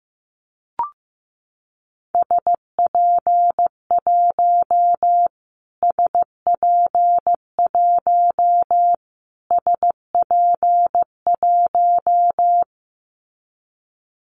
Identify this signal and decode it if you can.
Morse code, heard as a keyed tone, about 15 wpm, 700 Hz. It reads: SP1 SP1 SP1
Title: Callsign Prefixes for Top 25 DXCC - Repeated 3x - 15wpm